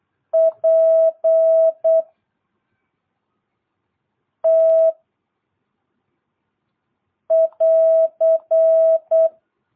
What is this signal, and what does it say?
Morse code, heard as a keyed tone, about 8 wpm, 660 Hz. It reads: P T <AR>
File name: jh8cbh / 2022cw